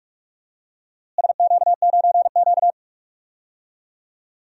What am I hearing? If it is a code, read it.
Morse code, heard as a keyed tone, about 46 wpm, 700 Hz. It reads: SQ9Q
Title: DXCC-100 Call Signs - 4 Characters - Repeated 3x - 45wpm